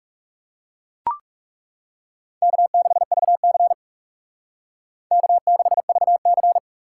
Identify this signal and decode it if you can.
Morse code, heard as a keyed tone, about 36 wpm, 700 Hz. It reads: K6VC K6VC
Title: Speed Racing - USA 1x2 Call Signs 30wpm